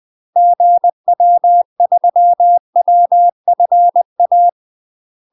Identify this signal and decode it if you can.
Morse code, heard as a keyed tone, about 20 wpm, 700 Hz. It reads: GW3WFA
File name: UK MASTER QSOs TRAINING / 20WPM / UK MASTER QSOs 0026